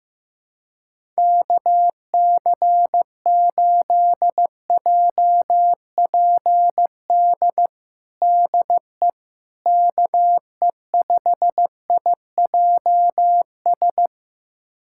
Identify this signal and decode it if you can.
Morse code, heard as a keyed tone, about 15 wpm, 700 Hz. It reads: KC8JPD DE KE5IJS
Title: USA Callsign DE Callsign - 6 Characters - 15wpm